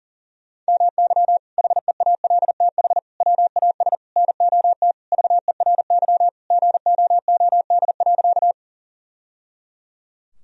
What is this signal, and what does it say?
Morse code, heard as a keyed tone, about 37 wpm, 700 Hz. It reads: MY HEALTH WAS NOT VERY GOOD.
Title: Sentences from Top 1000 Words - Review Part 3 - 40wpm